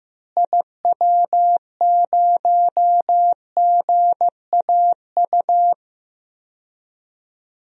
Morse code, heard as a keyed tone, about 16 wpm, 700 Hz. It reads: IW0GAU
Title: DXCC-100 Call Signs - 6 Characters - Repeated 3x - 15wpm